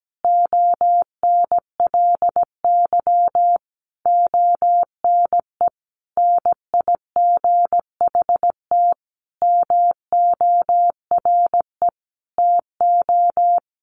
Morse code, heard as a keyed tone, about 17 wpm, 700 Hz. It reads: ONLY ONE NIGHT MORE TO
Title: Sentences from Top 300 Words 17wpm